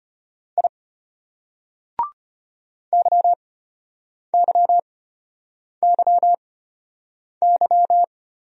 Morse code, heard as a keyed tone, about 34 wpm, spaced as for 8 wpm, 700 Hz. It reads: I YYYY